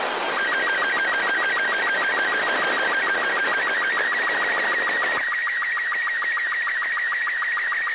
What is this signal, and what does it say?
SSTV (slow-scan television), Robot 36.
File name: ao-51-apollo-11